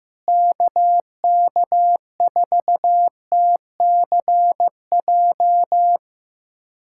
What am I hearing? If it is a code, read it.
Morse code, heard as a keyed tone, about 15 wpm, 700 Hz. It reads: KK4TCJ
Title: USA Callsign DE Callsign - 6 Characters - 15wpm